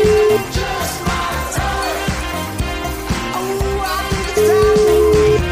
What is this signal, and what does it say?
Ringback tone (Europe, 425 Hz), with noise.